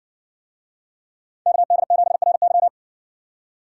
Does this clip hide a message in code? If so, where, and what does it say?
Yes: Morse code from 1.5 to 2.7 s, heard as a keyed tone, about 60 wpm, 700 Hz. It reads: KD7RY